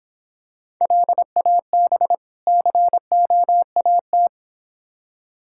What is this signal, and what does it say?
Morse code, heard as a keyed tone, about 26 wpm, 700 Hz. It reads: LAB COAT